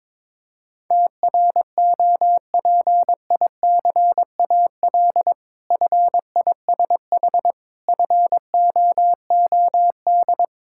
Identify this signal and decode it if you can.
Morse code, heard as a keyed tone, about 22 wpm, 690 Hz. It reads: TROPICAL FISH FOOD
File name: Difficult Words with Hints - 22wpm